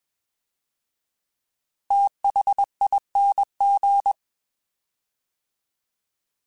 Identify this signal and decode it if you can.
Morse code, heard as a keyed tone, about 21 wpm, 800 Hz. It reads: THING
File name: TH / TH mp3s / TH Prefixes 3x2 Twice @20